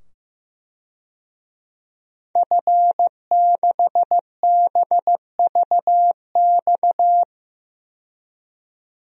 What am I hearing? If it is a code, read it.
Morse code, heard as a keyed tone, about 15 wpm, 700 Hz. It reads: F6BVX
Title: DXCC-100 Call Signs - 5 Characters - 15wpm